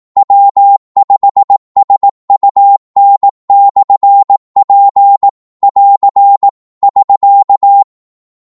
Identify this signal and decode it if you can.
Morse code, heard as a keyed tone, about 18 wpm, 800 Hz. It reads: W5SUN/P <AR> <SK>
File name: Morse / qso-18wpm